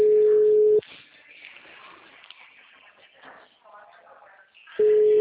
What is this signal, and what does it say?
Ringback tone (Europe, 425 Hz), with speech.
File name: out- -2012- - - .604